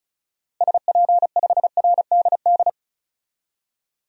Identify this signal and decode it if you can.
Morse code, heard as a keyed tone, about 36 wpm, 700 Hz. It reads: SP5RDD
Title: DXCC-100 Call Signs - 6 Characters - Repeated 2x - 35wpm